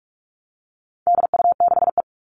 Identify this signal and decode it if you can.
Morse code, heard as a keyed tone, about 44 wpm, 700 Hz. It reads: DU6E
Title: All Cut Numbers - 4-Digit 45wpm